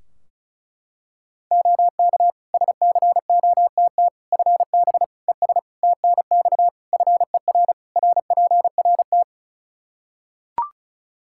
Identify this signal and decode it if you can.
Morse code, heard as a keyed tone, about 36 wpm, 700 Hz. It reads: OK SCOTT FB ES TNX FER RPRT
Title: Practice QSO - Exchange 2 - Info - 35wpm